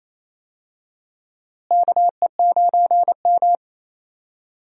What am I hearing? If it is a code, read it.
Morse code, heard as a keyed tone, about 28 wpm, 700 Hz. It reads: KE9M
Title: USA Call Signs - 4 Characters - Repeated 3x - 28wpm